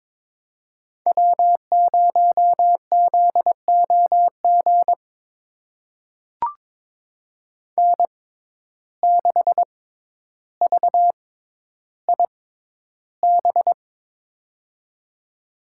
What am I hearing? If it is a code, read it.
Morse code, heard as a keyed tone, about 23 wpm, spaced as for 11 wpm, 700 Hz. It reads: W0ZOG N6VIB
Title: USA Call Signs - 5 Characters - 6x Character Spacing - 22wpm